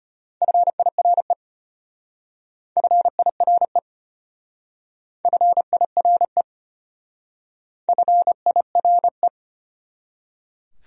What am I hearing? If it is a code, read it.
Morse code, heard as a keyed tone, about 32 wpm, 700 Hz. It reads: FIRE FIRE FIRE FIRE